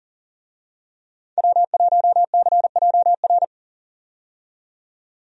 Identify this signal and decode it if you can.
Morse code, heard as a keyed tone, about 39 wpm, 700 Hz. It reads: W1CJR
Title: USA Call Signs - 5 Characters - 40wpm